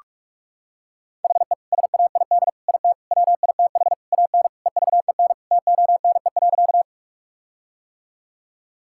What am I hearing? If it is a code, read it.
Morse code, heard as a keyed tone, about 41 wpm, 700 Hz. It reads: HE SAID IT WITH AN EVEN TONE.
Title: Sentences from Top 1600 Words 45wpm